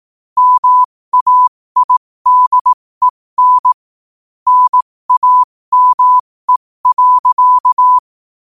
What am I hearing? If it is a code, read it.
Morse code, heard as a keyed tone, about 18 wpm, spaced as for 15 wpm, 1000 Hz. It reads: MAIDEN NAME.
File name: qotdfiles / 15wpm